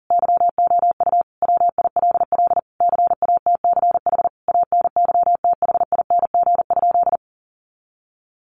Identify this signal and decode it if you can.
Morse code, heard as a keyed tone, about 37 wpm, 700 Hz. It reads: YOU WILL CATCH ANYTHING?
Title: Sentences from Top 1500 Words 40wpm